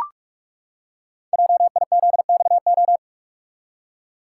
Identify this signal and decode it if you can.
Morse code, heard as a keyed tone, about 44 wpm, 700 Hz. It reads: JIZXO
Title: Code Groups - 5 Letters - 45wpm